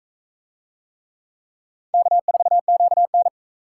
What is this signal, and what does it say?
Morse code, heard as a keyed tone, about 40 wpm, 700 Hz. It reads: K4QN